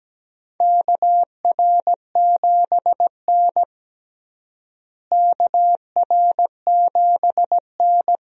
Morse code, heard as a keyed tone, about 17 wpm, 700 Hz. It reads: KR7N KR7N
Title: USA Call Signs - 4 Characters - Repeated 2x - 17wpm